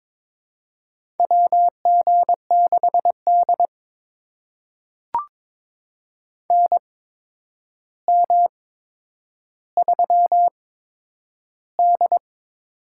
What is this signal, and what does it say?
Morse code, heard as a keyed tone, about 22 wpm, spaced as for 9 wpm, 700 Hz. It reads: WG6D NM3D
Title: USA Call Signs - 4 Characters - 8x Character Spacing - 22wpm